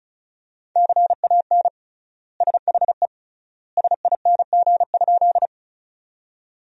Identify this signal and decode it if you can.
Morse code, heard as a keyed tone, about 36 wpm, 700 Hz. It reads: CAN SHE SING?
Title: 3x Word-Spacing - Sentences from Top 400 Words - 35wpm